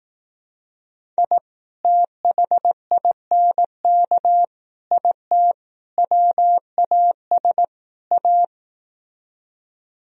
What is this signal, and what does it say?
Morse code, heard as a keyed tone, about 18 wpm, 700 Hz. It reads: I THINK IT WAS A
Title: Speed-Racing - Sets of 5 Words 15wpm